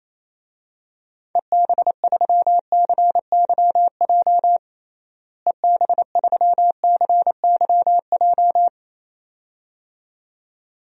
Morse code, heard as a keyed tone, about 28 wpm, 700 Hz. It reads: EB3CYJ EB3CYJ
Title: DXCC-100 Call Signs - 6 Characters - Repeated 2x - 28wpm